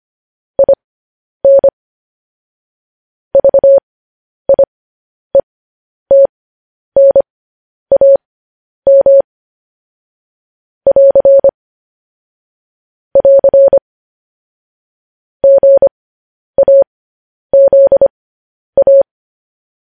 Morse code, heard as a keyed tone, about 25 wpm, spaced as for 10 wpm, 550 Hz. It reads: IN VIETNAM <AR> <AR> GAZA